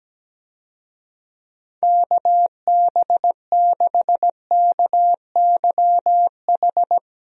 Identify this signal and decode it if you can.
Morse code, heard as a keyed tone, about 17 wpm, 700 Hz. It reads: KB6KYH
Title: USA Call Signs - 6 Characters - Repeated 2x - 17wpm